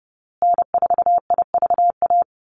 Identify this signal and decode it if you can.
Morse code, heard as a keyed tone, about 30 wpm, 700 Hz. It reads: N4IVA
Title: USA Call Signs Repeated 2x 30wpm